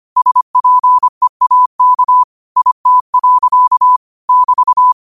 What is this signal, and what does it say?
Morse code, heard as a keyed tone, about 25 wpm, 1000 Hz. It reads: SPEAK IT. <BT>
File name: qotdfiles / 25wpm